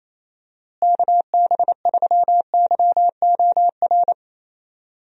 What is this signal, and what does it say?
Morse code, heard as a keyed tone, about 27 wpm, 700 Hz. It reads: KB3YOR
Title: USA Call Signs - 6 Characters - Repeated 2x - 28wpm